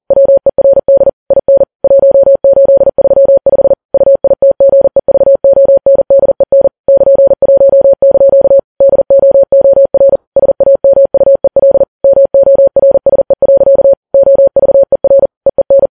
Morse code, heard as a keyed tone, about 40 wpm, 550 Hz. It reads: WERD IN 1835 UITGEVONDEN <KN>1) DOOR SAMUEL MORSE. OVER EEN
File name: Files Web tls / Morse on9fv / morse40kps